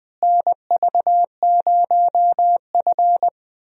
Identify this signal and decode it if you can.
Morse code, heard as a keyed tone, about 20 wpm, 700 Hz. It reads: NV0F
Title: USA Call Signs - 4 Characters - Repeated 3x - 20wpm